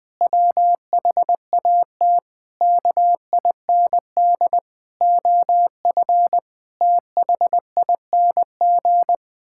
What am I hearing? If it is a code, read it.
Morse code, heard as a keyed tone, about 21 wpm, 700 Hz. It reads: WHAT KIND OF THING
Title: Sentences from Top 1000 Words - Review Part 6 - 20wpm